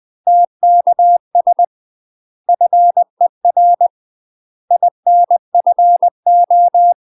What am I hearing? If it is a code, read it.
Morse code, heard as a keyed tone, about 20 wpm, 700 Hz. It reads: TKS FER INFO
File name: UK MASTER QSOs TRAINING/20WPM